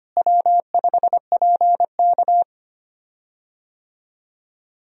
Morse code, heard as a keyed tone, about 25 wpm, 700 Hz. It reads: W5PK
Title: USA Call Signs - 4 Characters - Repeated 3x - 25wpm